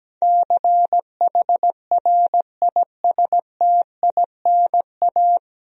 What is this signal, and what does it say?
Morse code, heard as a keyed tone, about 17 wpm, 700 Hz. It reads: CHRISTINA
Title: USA Common Names - 17wpm